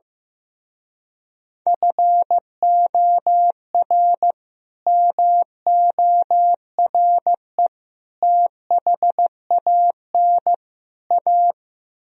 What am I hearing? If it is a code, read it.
Morse code, heard as a keyed tone, about 15 wpm, 700 Hz. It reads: FOR MORE THAN A
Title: Speed-Racing - Sets of 4 Words 15wpm